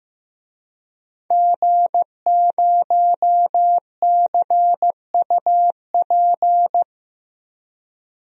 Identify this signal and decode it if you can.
Morse code, heard as a keyed tone, about 15 wpm, 700 Hz. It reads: G0CUP
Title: DXCC-100 Call Signs - 5 Characters - Repeated 3x - 15wpm